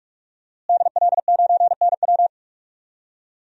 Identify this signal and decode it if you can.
Morse code, heard as a keyed tone, about 46 wpm, 700 Hz. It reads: DL9NW
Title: DXCC-100 Call Signs - 5 Characters - Repeated 2x - 45wpm